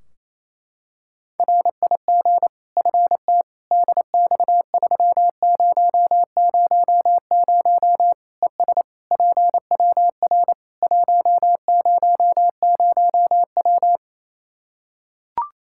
Morse code, heard as a keyed tone, about 28 wpm, 700 Hz. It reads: RIG FT DX3000 ES PWR 100W
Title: K1USN SST QSO Practice 28wpm